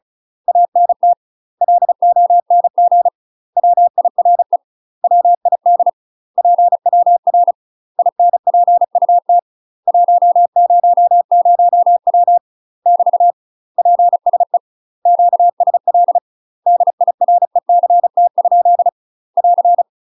Morse code, heard as a keyed tone, about 36 wpm, 700 Hz. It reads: ANT LONG WIRE WID PWR INPUT 100W <BT> PSE QSL DIRECT? <AR>
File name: UK MASTER QSOs TRAINING / 35WPM / UK MASTER QSOs 0033